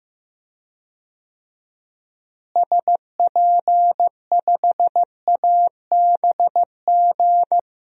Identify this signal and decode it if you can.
Morse code, heard as a keyed tone, about 15 wpm, 700 Hz. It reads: SP5ABG
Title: DXCC-100 Call Signs - 6 Characters - Repeated 2x - 15wpm